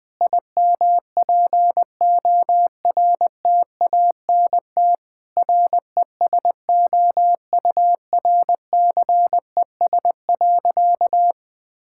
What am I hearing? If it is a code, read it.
Morse code, heard as a keyed tone, about 19 wpm, 700 Hz. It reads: IMPORTANT RESOURCES.
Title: Sentences with 5-Word Phrases - 20wpm